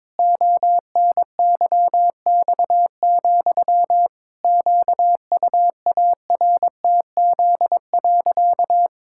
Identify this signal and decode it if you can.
Morse code, heard as a keyed tone, about 22 wpm, 700 Hz. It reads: ONYX, QUARTZ.
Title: Pangram Challenge v2 - 22wpm